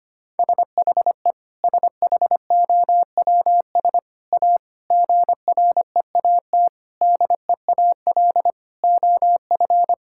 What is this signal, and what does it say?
Morse code, heard as a keyed tone, about 25 wpm, 700 Hz. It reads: SHE SHOWS A GREAT DEAL OF
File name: practice / 25wpm / Sentences with 4-Word Phrases - 25wpm